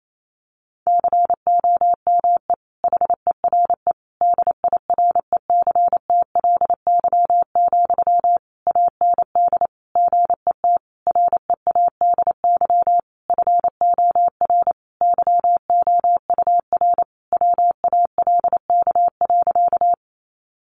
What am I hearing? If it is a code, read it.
Morse code, heard as a keyed tone, about 28 wpm, 700 Hz. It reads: COME HERE DIRECTLY, AND GET READY FOR YOUR WALK.
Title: Sentences from Top 800 Words 28wpm